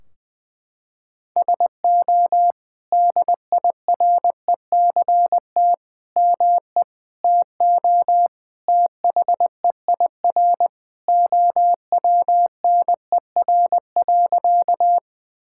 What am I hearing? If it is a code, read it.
Morse code, heard as a keyed tone, about 20 wpm, 700 Hz. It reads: SO DIRECT ME TO THEIR OWNER.
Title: Sentences from Top 1600 Words 20wpm